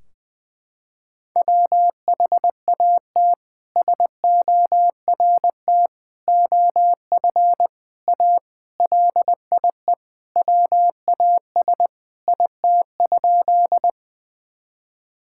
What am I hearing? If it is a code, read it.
Morse code, heard as a keyed tone, about 20 wpm, 700 Hz. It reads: WHAT SORT OF A LIE WAS IT?
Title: Sentences from Top 1600 Words 20wpm